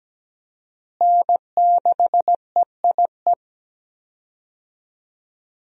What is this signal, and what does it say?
Morse code, heard as a keyed tone, about 17 wpm, 700 Hz. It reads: N6EIE
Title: USA Call Signs - 5 Characters - Repeated 3x - 17wpm